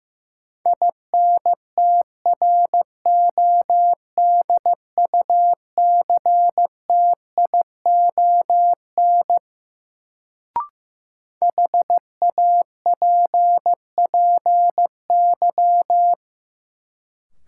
Morse code, heard as a keyed tone, about 15 wpm, 700 Hz. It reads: INTRODUCTION HAPPY
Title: Sentences from Top 1000 Words - New Word Drill - 15wpm